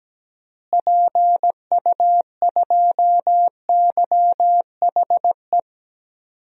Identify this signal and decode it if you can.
Morse code, heard as a keyed tone, about 17 wpm, 700 Hz. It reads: PU2YHE
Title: DXCC-100 Call Signs - 6 Characters - Repeated 2x - 17wpm